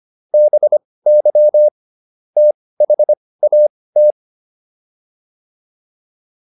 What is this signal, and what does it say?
Morse code, heard as a keyed tone, about 25 wpm, spaced as for 18 wpm, 600 Hz. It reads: BY THAT